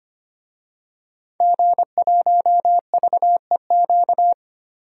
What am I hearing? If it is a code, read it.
Morse code, heard as a keyed tone, about 25 wpm, 700 Hz. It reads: G1VEQ
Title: DXCC-100 Call Signs - 5 Characters - 25wpm